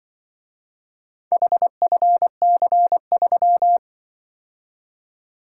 Morse code, heard as a keyed tone, about 24 wpm, 700 Hz. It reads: HFC3
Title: Speed-Racing - Four Letter-Number Combinations - 20wpm